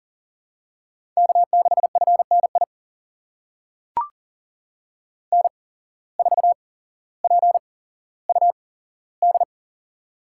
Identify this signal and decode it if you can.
Morse code, heard as a keyed tone, about 40 wpm, spaced as for 17 wpm, 700 Hz. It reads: K6FNI N4PUD
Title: USA Call Signs - 5 Characters - 8x Character Spacing - 40wpm